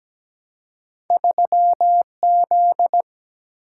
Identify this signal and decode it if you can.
Morse code, heard as a keyed tone, about 17 wpm, 700 Hz. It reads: 3Z
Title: Single-Letter Single-Number 17wpm